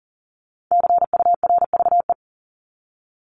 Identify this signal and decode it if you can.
Morse code, heard as a keyed tone, about 40 wpm, 700 Hz. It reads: CURVE
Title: Top 1501-1750 Words 40wpm